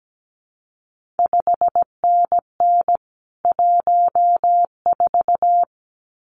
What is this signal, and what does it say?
Morse code, heard as a keyed tone, about 17 wpm, 700 Hz. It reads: 5NN 14
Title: CQ WW DX Contest Exchange - 17wpm